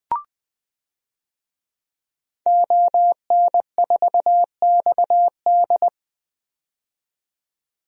Morse code, heard as a keyed tone, about 21 wpm, 700 Hz. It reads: ON4XD
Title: DXCC-100 Call Signs - 5 Characters - 20wpm